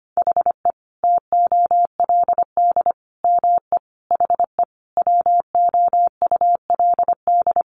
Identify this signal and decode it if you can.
Morse code, heard as a keyed tone, about 25 wpm, 700 Hz. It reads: HE TOLD ME HE WOULD
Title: Sentences from Top 1300 Words 25wpm